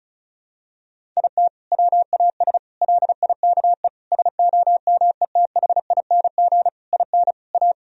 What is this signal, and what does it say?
Morse code, heard as a keyed tone, about 34 wpm, 700 Hz. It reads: IT WAS LIKE SOMETHING IN A
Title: Sentences from Top 1000 Words - Review Part 1 - 35wpm